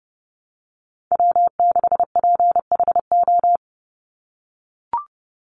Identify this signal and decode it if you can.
Morse code, heard as a keyed tone, about 30 wpm, 700 Hz. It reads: W6PHO
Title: Call Signs - 30wpm